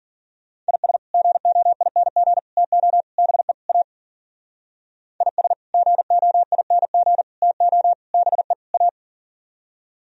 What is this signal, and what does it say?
Morse code, heard as a keyed tone, about 52 wpm, 700 Hz. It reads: IS GOING TO BE A IS GOING TO BE A